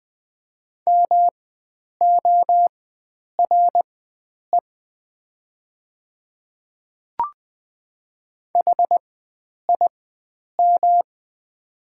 Morse code, heard as a keyed tone, about 20 wpm, spaced as for 9 wpm, 700 Hz. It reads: MORE HIM
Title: 4x Character Spacing - Top 100 Words - 20wpm